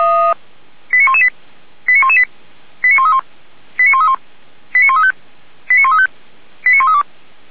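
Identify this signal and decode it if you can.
Five-tone selective call (ZVEI), heard as quick runs of five tones, tones of about 70 ms. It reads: 89118 89118 89131 89131 89135 89135 89132
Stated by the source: ZVEI five-tone call received by SDR, from server of feuerwehr-orlamuende.org (fire brigade Orlamünde).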